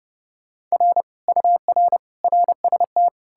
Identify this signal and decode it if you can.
Morse code, heard as a keyed tone, about 37 wpm, 700 Hz. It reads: R UR RST
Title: Speed-Racing Common QSO Phrases - 30wpm